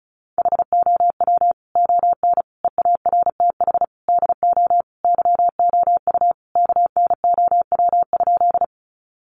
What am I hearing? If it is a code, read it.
Morse code, heard as a keyed tone, about 34 wpm, 700 Hz. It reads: HOW ON EARTH DO YOU KNOW?
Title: Sentences from Top 200 Words 35wpm